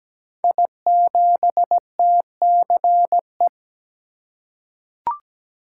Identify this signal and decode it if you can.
Morse code, heard as a keyed tone, about 17 wpm, 700 Hz. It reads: I7TCE
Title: Code Groups - 5 Letter:Numbers - 17wpm